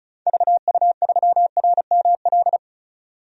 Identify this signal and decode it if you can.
Morse code, heard as a keyed tone, about 36 wpm, 700 Hz. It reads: VU3RML
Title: DXCC-100 Call Signs - 6 Characters - 35wpm